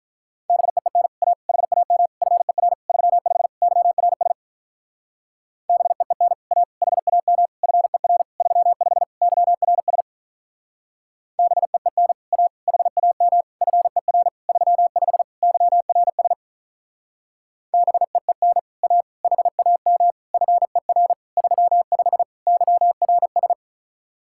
Morse code, heard as a keyed tone, about 45 wpm, 700 Hz. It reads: BEEN A HAM FER 35 YRS BEEN A HAM FER 35 YRS BEEN A HAM FER 35 YRS BEEN A HAM FER 35 YRS